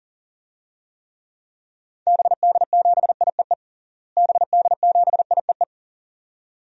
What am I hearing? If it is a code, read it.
Morse code, heard as a keyed tone, about 40 wpm, 700 Hz. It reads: BD7IEE BD7IEE
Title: DXCC-100 Call Signs - 6 Characters - Repeated 2x - 40wpm